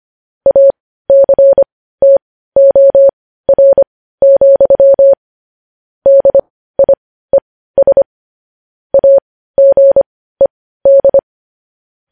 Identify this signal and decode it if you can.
Morse code, heard as a keyed tone, about 25 wpm, spaced as for 15 wpm, 550 Hz. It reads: ACTOR, DIES AGED
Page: news headlines in morse code at 15 wpm